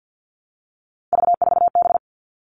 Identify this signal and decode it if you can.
Morse code, heard as a keyed tone, about 50 wpm, 700 Hz. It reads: V4B
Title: All Cut Numbers - 3-Digit 50wpm